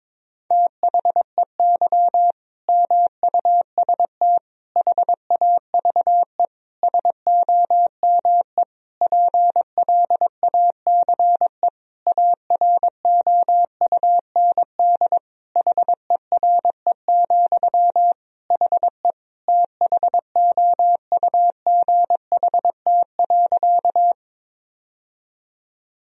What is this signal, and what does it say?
Morse code, heard as a keyed tone, about 22 wpm, 700 Hz. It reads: THEY MUST HAVE SOME PLACE AROUND HERE, HE THOUGHT.
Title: Sentences from Top 1000 Words - Review Part 3 - 22wpm